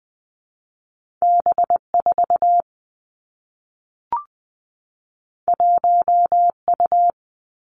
Morse code, heard as a keyed tone, about 20 wpm, 700 Hz. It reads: B 4 1 U
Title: Two Challenging Letter-Number Combinations 20wpm